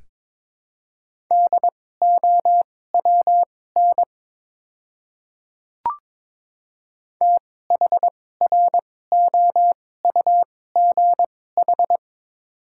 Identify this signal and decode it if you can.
Morse code, heard as a keyed tone, about 22 wpm, spaced as for 16 wpm, 700 Hz. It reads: DOWN THROUGH
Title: 2x Character Spacing - Top 300 Words - 22wpm